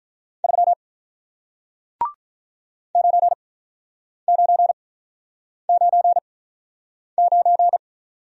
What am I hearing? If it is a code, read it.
Morse code, heard as a keyed tone, about 47 wpm, spaced as for 9 wpm, 700 Hz. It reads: 3 9999